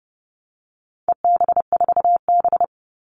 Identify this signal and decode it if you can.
Morse code, heard as a keyed tone, about 30 wpm, 700 Hz. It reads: EB4B
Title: All Cut Numbers - 4-Digit 30wpm